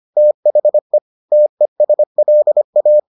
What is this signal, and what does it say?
Morse code, heard as a keyed tone, about 25 wpm, 600 Hz. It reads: THE TESLA